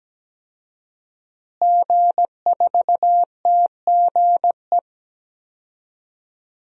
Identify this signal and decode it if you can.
Morse code, heard as a keyed tone, about 17 wpm, 700 Hz. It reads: G4TGE
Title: DXCC-100 Call Signs - 5 Characters - Repeated 3x - 17wpm